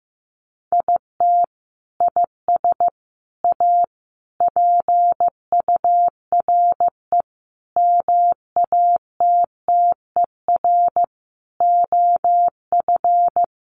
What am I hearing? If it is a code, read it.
Morse code, heard as a keyed tone, about 15 wpm, 700 Hz. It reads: IT IS A PURE MATTER OF
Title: Sentences from Top 1500 Words 15wpm